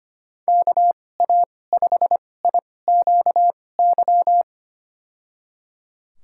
Morse code, heard as a keyed tone, about 25 wpm, spaced as for 18 wpm, 700 Hz. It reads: KA5IQY
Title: USA Call Signs - 2x Character Spacing - 25wpm